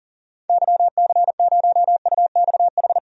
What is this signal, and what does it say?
Morse code, heard as a keyed tone, about 40 wpm, 700 Hz. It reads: YC0UXH
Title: DXCC-100 Call Signs - 6 Characters - 40wpm